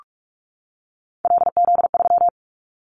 Morse code, heard as a keyed tone, about 44 wpm, 700 Hz. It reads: LZ3
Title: Callsign Prefixes for Top 25 DXCC - 45wpm